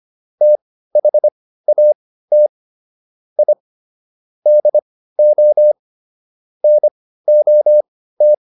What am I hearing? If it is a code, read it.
Morse code, heard as a keyed tone, about 25 wpm, spaced as for 15 wpm, 600 Hz. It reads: THAT I DO NOT